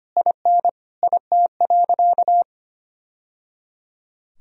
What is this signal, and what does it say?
Morse code, heard as a keyed tone, about 25 wpm, 700 Hz. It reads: IN IT.
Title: Sentences from Top 1000 Words - Review Part 2 - 25wpm